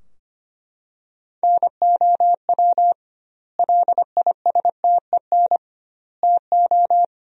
Morse code, heard as a keyed tone, about 25 wpm, 700 Hz. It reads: NOW LISTEN TO
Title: 2x Word-Spacing - Sentences from Top 400 Words - 25wpm